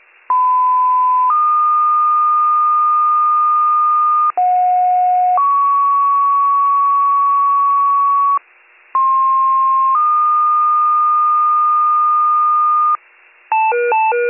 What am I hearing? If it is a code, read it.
Two-tone page, heard as a one-second tone, then a three-second tone, about 1010 then 1220 Hz, about 730 then 1120 Hz, about 1060 then 1220 Hz. It is heard from 0.3 to 13.0 s.